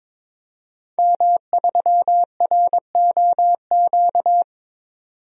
Morse code, heard as a keyed tone, about 22 wpm, 700 Hz. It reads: M3ROQ
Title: DXCC-100 Call Signs - 5 Characters - 22wpm